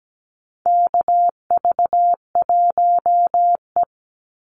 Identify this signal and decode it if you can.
Morse code, heard as a keyed tone, about 17 wpm, 700 Hz. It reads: KV1E
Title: Call Signs - 17wpm